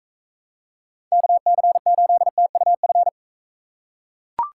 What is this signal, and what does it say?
Morse code, heard as a keyed tone, about 39 wpm, 700 Hz. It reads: KC8TUF